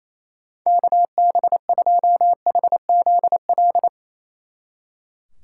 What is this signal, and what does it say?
Morse code, heard as a keyed tone, about 33 wpm, 700 Hz. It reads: KB2HZL
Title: Speed Racing - USA Call Signs 28wpm